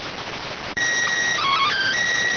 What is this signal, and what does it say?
SSTV (slow-scan television), PD 290.